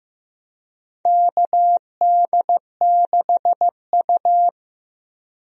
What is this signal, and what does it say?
Morse code, heard as a keyed tone, about 15 wpm, 700 Hz. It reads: KD6U
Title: USA Call Signs - 4 Characters - Repeated 3x - 15wpm